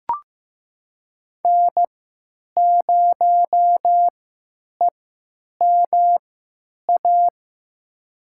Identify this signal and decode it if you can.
Morse code, heard as a keyed tone, about 15 wpm, spaced as for 8 wpm, 700 Hz. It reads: N0EMA
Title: USA Call Signs - 5 Characters - 3x Character Spacing - 15wpm